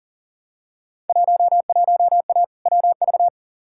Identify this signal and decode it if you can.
Morse code, heard as a keyed tone, about 40 wpm, 700 Hz. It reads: 11A WV
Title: Field Day Exchanges Encore - 40wpm